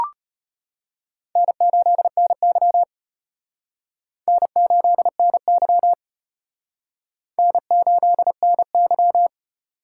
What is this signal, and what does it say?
Morse code, heard as a keyed tone, about 32 wpm, 700 Hz. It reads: N8NY N8NY N8NY